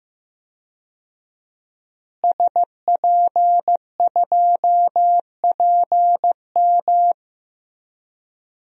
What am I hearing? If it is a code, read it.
Morse code, heard as a keyed tone, about 16 wpm, 700 Hz. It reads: SP2PM
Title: DXCC-100 Call Signs - 5 Characters - Repeated 3x - 15wpm